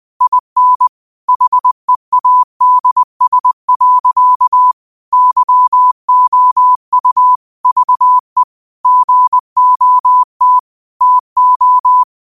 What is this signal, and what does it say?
Morse code, heard as a keyed tone, about 20 wpm, 1000 Hz. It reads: IN HEADS. YOU VE GOT TO
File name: qotdfiles / 20wpm